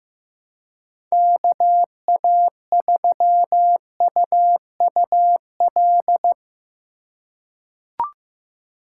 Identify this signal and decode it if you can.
Morse code, heard as a keyed tone, about 16 wpm, 700 Hz. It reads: KA3UUL
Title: USA Call Signs - 6 Characters - 4x Character Spacing - 15wpm